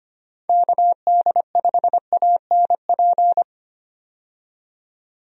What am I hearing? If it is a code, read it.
Morse code, heard as a keyed tone, about 25 wpm, 700 Hz. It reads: KD5ANP
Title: USA Call Signs - 6 Characters - 25wpm